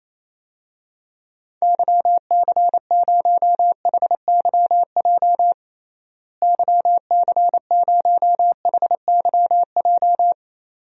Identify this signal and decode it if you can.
Morse code, heard as a keyed tone, about 28 wpm, 700 Hz. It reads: YC0HYJ YC0HYJ
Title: DXCC-100 Call Signs - 6 Characters - Repeated 2x - 28wpm